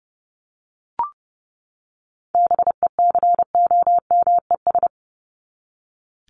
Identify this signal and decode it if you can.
Morse code, heard as a keyed tone, about 30 wpm, 700 Hz. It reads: BECOMES
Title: ES:S Suffix - Top 100 Words with Base Words 30wpm